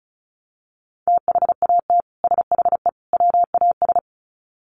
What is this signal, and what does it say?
Morse code, heard as a keyed tone, about 34 wpm, 700 Hz. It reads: THAT SHE WAS
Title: Sets of 3 Words 35wpm